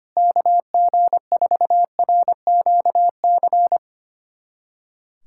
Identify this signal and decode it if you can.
Morse code, heard as a keyed tone, about 25 wpm, 700 Hz. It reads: KG4RQC
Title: USA Callsign DE Callsign 25wpm